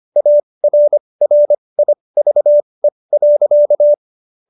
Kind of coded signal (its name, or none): Morse code